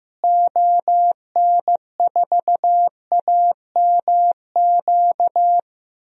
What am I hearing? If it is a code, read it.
Morse code, heard as a keyed tone, about 15 wpm, 700 Hz. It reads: ON4AMQ
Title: DXCC-100 Call Signs - 6 Characters - Repeated 2x - 15wpm